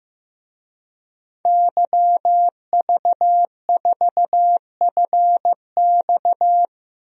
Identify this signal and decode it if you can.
Morse code, heard as a keyed tone, about 15 wpm, 700 Hz. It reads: YV4FX
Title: DXCC-100 Call Signs - 5 Characters - Repeated 3x - 15wpm